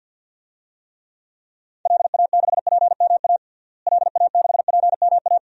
Morse code, heard as a keyed tone, about 50 wpm, 700 Hz. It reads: LA6PMA LA6PMA
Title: DXCC-100 Call Signs - 6 Characters - Repeated 2x - 50wpm